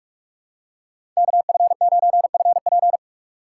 Morse code, heard as a keyed tone, about 46 wpm, 690 Hz. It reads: KF9FP
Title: USA Call Signs - 4x Character Spacing - 45wpm